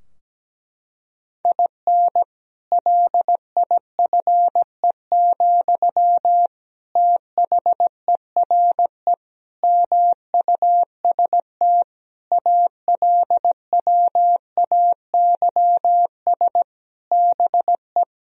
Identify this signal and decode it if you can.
Morse code, heard as a keyed tone, about 17 wpm, 700 Hz. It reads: IN LIFE, THERE MUST ALWAYS BE
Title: Sentences from Top 1000 Words - Review Part 4 - 17wpm